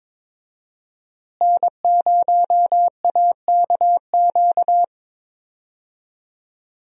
Morse code, heard as a keyed tone, about 22 wpm, 700 Hz. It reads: N0AKQ
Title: USA Call Signs - 5 Characters - Repeated 3x - 22wpm